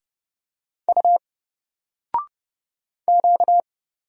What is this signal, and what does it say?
Morse code, heard as a keyed tone, about 31 wpm, 700 Hz. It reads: U Q
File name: Single-Letter-Number-30wpm